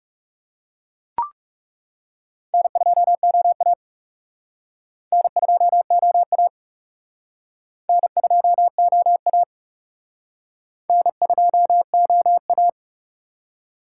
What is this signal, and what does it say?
Morse code, heard as a keyed tone, about 35 wpm, 700 Hz. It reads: N2OA N2OA N2OA N2OA
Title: Speed Racing - USA 1x2 Call Signs 30wpm